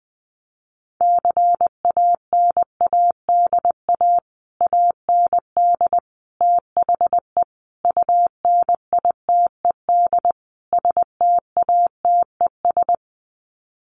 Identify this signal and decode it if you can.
Morse code, heard as a keyed tone, about 20 wpm, 700 Hz. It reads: CANADA AND THE UNITED STATES
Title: Sets of 5 Words Encore 20wpm